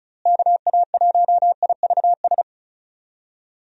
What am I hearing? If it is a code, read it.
Morse code, heard as a keyed tone, about 36 wpm, 700 Hz. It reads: KA1IVS
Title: USA Call Signs - 6 Characters - Repeated 2x - 35wpm